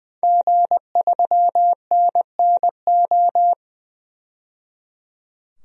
Morse code, heard as a keyed tone, about 21 wpm, 700 Hz. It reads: G3NNO
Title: DXCC-100 Call Signs - 5 Characters - 20wpm